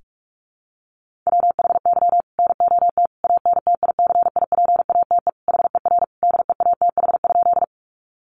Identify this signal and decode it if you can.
Morse code, heard as a keyed tone, about 44 wpm, 700 Hz. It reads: WHY NOT ANTICIPATE HER DEATH?
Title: Sentences from Top 1500 Words 45wpm